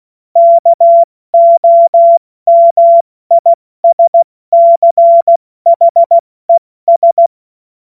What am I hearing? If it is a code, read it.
Morse code, heard as a keyed tone, about 16 wpm, 680 Hz. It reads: KOMISCHES
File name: Morsekurs / Witze / Witz 05 Dezember 2023 70 F80